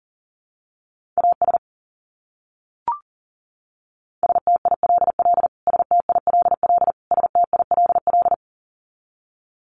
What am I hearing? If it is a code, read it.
Morse code, heard as a keyed tone, about 40 wpm, 700 Hz. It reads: AS STILL STILL STILL
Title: Top 300 Words Repeated 40wpm